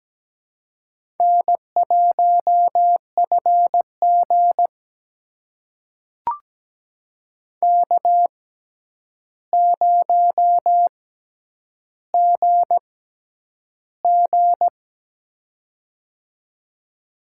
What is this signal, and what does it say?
Morse code, heard as a keyed tone, about 17 wpm, spaced as for 9 wpm, 700 Hz. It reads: N1FG K0GG
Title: USA Call Signs - 4 Characters - 6x Character Spacing - 17wpm